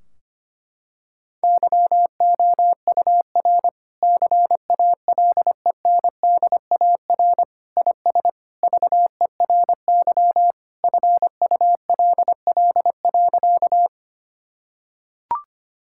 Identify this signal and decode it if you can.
Morse code, heard as a keyed tone, about 25 wpm, 700 Hz. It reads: YOUR CALENDAR IS VERY FULL.
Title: Sentences from Top 1600 Words 25wpm